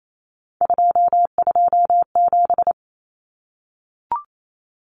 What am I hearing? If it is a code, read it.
Morse code, heard as a keyed tone, about 28 wpm, 700 Hz. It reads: 227
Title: Common Cut Numbers - 3-Digit 28wpm